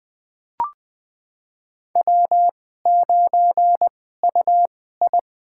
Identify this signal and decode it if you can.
Morse code, heard as a keyed tone, about 20 wpm, spaced as for 14 wpm, 700 Hz. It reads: W9UI
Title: USA Call Signs - 4 Characters - 2x Character Spacing - 20wpm